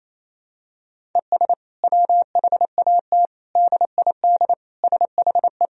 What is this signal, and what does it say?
Morse code, heard as a keyed tone, about 28 wpm, 700 Hz. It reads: ES WHAT DID SHE
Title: Sentences from Top 200 Words and Abbreviations 28wpm